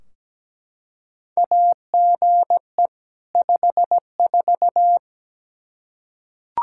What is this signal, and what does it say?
Morse code, heard as a keyed tone, about 17 wpm, 700 Hz. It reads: AGE 54
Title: K1USN SST QSO Practice 17wpm